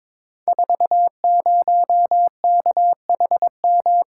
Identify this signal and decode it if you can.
Morse code, heard as a keyed tone, about 22 wpm, 700 Hz. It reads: 40KHM
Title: Code Groups - 5 Letter:Numbers - 22wpm